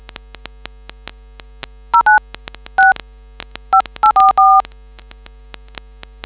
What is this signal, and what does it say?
DTMF, heard as telephone touch tones, tones of about 140 ms. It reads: 09 6 5844